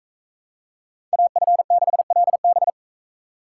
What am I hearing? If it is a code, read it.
Morse code, heard as a keyed tone, about 42 wpm, 700 Hz. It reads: AF6LB